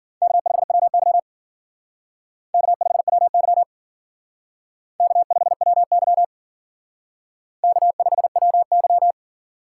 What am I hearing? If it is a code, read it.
Morse code, heard as a keyed tone, about 53 wpm, 700 Hz. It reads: K5WY K5WY K5WY K5WY